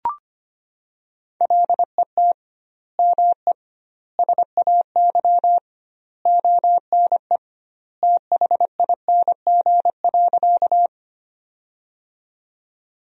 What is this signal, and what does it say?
Morse code, heard as a keyed tone, about 25 wpm, 700 Hz. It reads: LET ME SAY ONE THING.
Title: 2x Word-Spacing - Sentences from Top 300 Words - 25wpm